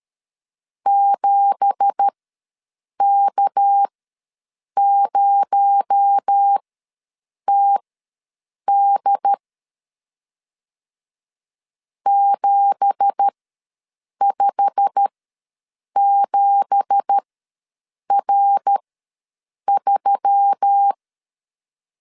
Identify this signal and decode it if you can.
Morse code, heard as a keyed tone, about 13 wpm, spaced as for 7 wpm, 800 Hz. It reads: 7K0TD 757R3
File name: LEZIONI / 9 2867 ws 12wpm 3431